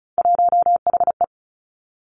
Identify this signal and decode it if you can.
Morse code, heard as a keyed tone, about 36 wpm, 700 Hz. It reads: 1HE
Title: Three Challenging Letter-Number Combinations 35wpm